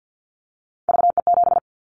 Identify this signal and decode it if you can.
Morse code, heard as a keyed tone, about 50 wpm, 700 Hz. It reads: VE7
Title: Callsign Prefixes for Top 100 DXCC - 50wpm